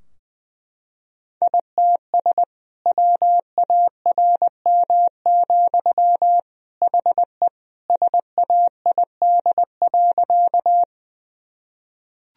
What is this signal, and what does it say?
Morse code, heard as a keyed tone, about 20 wpm, 700 Hz. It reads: ITS WARM, HE SAID.
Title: Sentences from Top 1000 Words - Review Part 2 - 20wpm